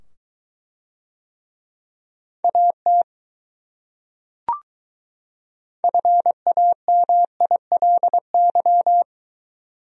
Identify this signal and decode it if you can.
Morse code, heard as a keyed tone, about 22 wpm, 700 Hz. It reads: AT FAMILY